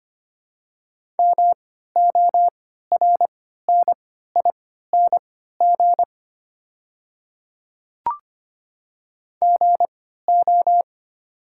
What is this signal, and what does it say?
Morse code, heard as a keyed tone, about 25 wpm, spaced as for 14 wpm, 690 Hz. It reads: MORNING GO
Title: 3x Character Spacing - Top 500 Words - 25wpm